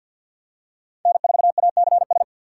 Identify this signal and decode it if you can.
Morse code, heard as a keyed tone, about 52 wpm, 700 Hz. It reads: N4ACS
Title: USA Call Signs - 6x Character Spacing - 50wpm